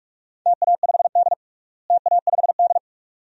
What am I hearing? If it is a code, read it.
Morse code, heard as a keyed tone, about 46 wpm, 700 Hz. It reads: TA5D TA5D
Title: DXCC-100 Call Signs - 4 Characters - Repeated 2x - 45wpm